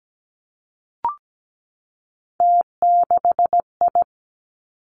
Morse code, heard as a keyed tone, about 17 wpm, 700 Hz. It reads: T6I
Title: Three Challenging Letter-Number Combinations 17wpm